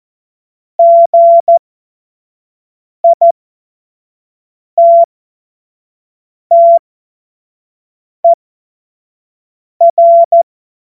Morse code, heard as a keyed tone, about 14 wpm, spaced as for 5 wpm, 680 Hz. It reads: GITTER
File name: Morsekurs / Witze / Witz 01. Juni 2021, 25 F70